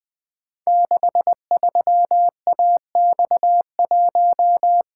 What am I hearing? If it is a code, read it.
Morse code, heard as a keyed tone, about 21 wpm, 700 Hz. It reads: 63AX1
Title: Code Groups - 5 Letter:Numbers - 20wpm